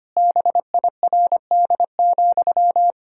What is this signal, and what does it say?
Morse code, heard as a keyed tone, about 25 wpm, 700 Hz. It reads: BIRD,
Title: Sentences from Top 1000 Words - Review Part 2 - 25wpm